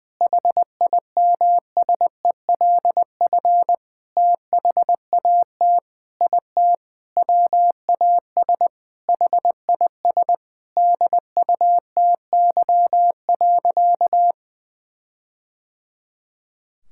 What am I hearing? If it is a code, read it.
Morse code, heard as a keyed tone, about 20 wpm, 700 Hz. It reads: HIMSELF THAT IT WAS HIS DUTY.
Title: Sentences from Top 1600 Words 20wpm